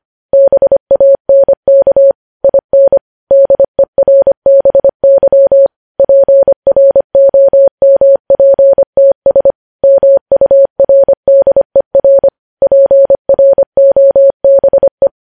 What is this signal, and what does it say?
Morse code, heard as a keyed tone, about 25 wpm, 550 Hz. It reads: BANK IN DERBY PROMPTS MURDER PROBE
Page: news headlines in morse code at 25 wpm